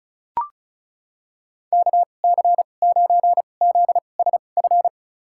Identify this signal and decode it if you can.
Morse code, heard as a keyed tone, about 36 wpm, spaced as for 25 wpm, 700 Hz. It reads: KC9ZSF
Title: USA Call Signs - 6 Characters - 2x Character Spacing - 35wpm